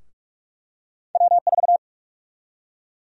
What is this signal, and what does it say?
Morse code, heard as a keyed tone, about 42 wpm, 700 Hz. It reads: W4
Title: USA Call Sign Prefixes - 45wpm